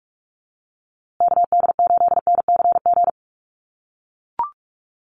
Morse code, heard as a keyed tone, about 44 wpm, 690 Hz. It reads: KD8NCG
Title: USA Call Signs Repeated 3x 45wpm